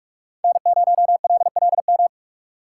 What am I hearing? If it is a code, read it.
Morse code, heard as a keyed tone, about 42 wpm, 700 Hz. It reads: N0LLM